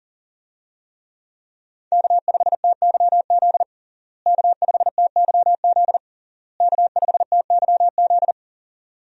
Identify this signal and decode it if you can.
Morse code, heard as a keyed tone, about 37 wpm, 700 Hz. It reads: K5TYZ K5TYZ K5TYZ
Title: USA Call Signs - 5 Characters - Repeated 3x - 40wpm